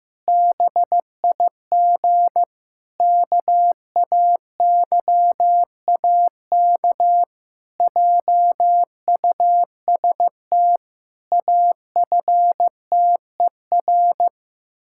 Morse code, heard as a keyed tone, about 15 wpm, 700 Hz. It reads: BIG KAYAK JUST AFTER
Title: Pangram Challenge v2 - 15wpm